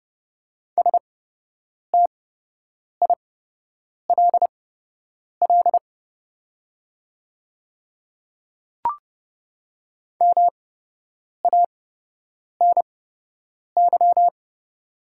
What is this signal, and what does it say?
Morse code, heard as a keyed tone, about 30 wpm, spaced as for 8 wpm, 700 Hz. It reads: STILL MANY
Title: 8x Character Spacing - Top 300 Words - 30wpm